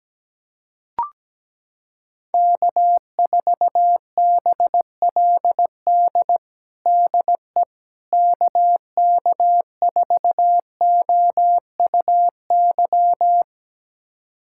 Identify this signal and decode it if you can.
Morse code, heard as a keyed tone, about 17 wpm, 700 Hz. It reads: K4BLD DE KK4OUY
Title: USA Callsign DE Callsign 17wpm